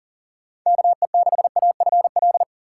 Morse code, heard as a keyed tone, about 40 wpm, 700 Hz. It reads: KE6AFL
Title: USA Call Signs - 6 Characters - 4x Character Spacing - 40wpm